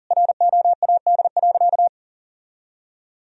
Morse code, heard as a keyed tone, about 40 wpm, 700 Hz. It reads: ROAD.
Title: Sentences from Top 1000 Words - Review Part 2 - 40wpm